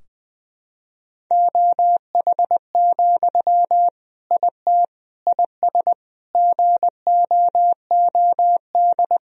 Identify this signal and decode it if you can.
Morse code, heard as a keyed tone, about 20 wpm, 700 Hz. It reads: OH, IT IS GOOD
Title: Sentences from Top 1000 Words - Review Part 3 - 20wpm